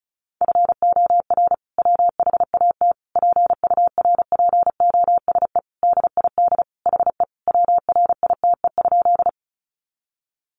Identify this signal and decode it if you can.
Morse code, heard as a keyed tone, about 34 wpm, 700 Hz. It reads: FOR WHAT PURPOSE DID HE WRITE?
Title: Sentences from Top 800 Words 35wpm